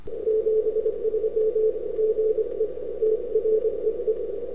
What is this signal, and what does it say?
Morse code, heard as a keyed tone, about 22 wpm, 450 Hz. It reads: 73 GE TREE
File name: n6tr / 12Oct2009-160